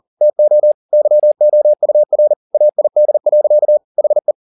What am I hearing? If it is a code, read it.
Morse code, heard as a keyed tone, about 40 wpm, 600 Hz. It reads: TO YOUR AID. HE